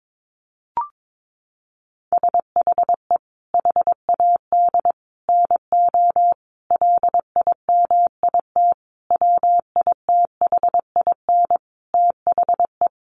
Morse code, heard as a keyed tone, about 22 wpm, 700 Hz. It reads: SHE HAD NO LIMIT WITHIN THE
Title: Sentences from Top 1500 Words 22wpm